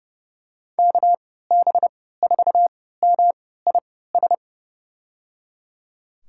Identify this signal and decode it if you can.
Morse code, heard as a keyed tone, about 30 wpm, spaced as for 17 wpm, 700 Hz. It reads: KB4MIS
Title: USA Call Signs - 3x Character Spacing - 30wpm